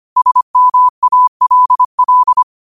Morse code, heard as a keyed tone, about 25 wpm, 1000 Hz. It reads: SMALL
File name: qotdfiles / 25wpm